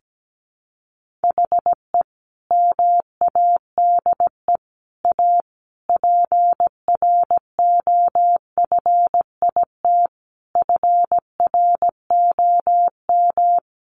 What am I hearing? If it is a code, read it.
Morse code, heard as a keyed tone, about 17 wpm, 700 Hz. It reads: HE MADE A PROFIT FROM
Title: Sentences from Top 800 Words 17wpm